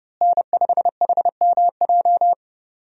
Morse code, heard as a keyed tone, about 30 wpm, 700 Hz. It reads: N5HMJ
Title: USA Call Signs - 5 Characters - Repeated 3x - 30wpm